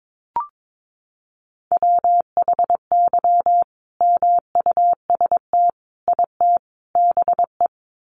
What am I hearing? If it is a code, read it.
Morse code, heard as a keyed tone, about 22 wpm, 700 Hz. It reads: WHY MUST IT BE
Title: Sentences from Top 600 Words 22wpm